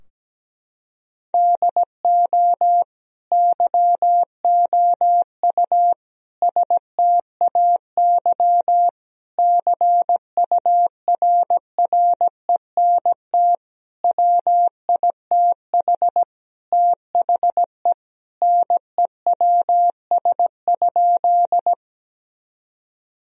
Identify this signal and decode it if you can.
Morse code, heard as a keyed tone, about 17 wpm, 700 Hz. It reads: DO YOU STAY CURRENT WITH THE NEWS?
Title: Sentences from Top 1000 Words - Review Part 4 - 17wpm